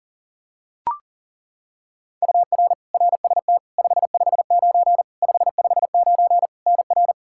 Morse code, heard as a keyed tone, about 40 wpm, 700 Hz. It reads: UR RST 559 559 NR
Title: K1USN SST QSO Practice 40wpm